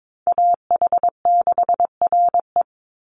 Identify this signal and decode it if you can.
Morse code, heard as a keyed tone, about 22 wpm, 700 Hz. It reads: AH6RE
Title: Call Signs - 22wpm